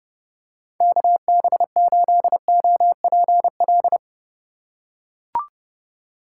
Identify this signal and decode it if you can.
Morse code, heard as a keyed tone, about 32 wpm, 700 Hz. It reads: KB8OPL
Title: USA Call Signs - 6 Characters - 8x Character Spacing - 30wpm